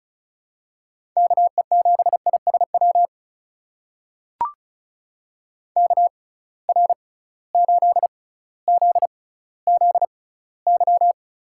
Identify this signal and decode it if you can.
Morse code, heard as a keyed tone, about 36 wpm, spaced as for 18 wpm, 690 Hz. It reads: KE7ISW KR8ZZY
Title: USA Call Signs - 6 Characters - 6x Character Spacing - 35wpm